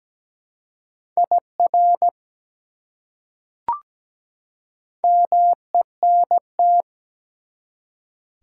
Morse code, heard as a keyed tone, about 17 wpm, 700 Hz. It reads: IR MENT
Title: Prefixes and Suffixes 17wpm